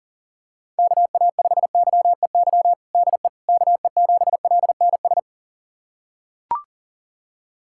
Morse code, heard as a keyed tone, about 40 wpm, 700 Hz. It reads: KA5YEY DE KE7LNS
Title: USA Callsign DE Callsign 40wpm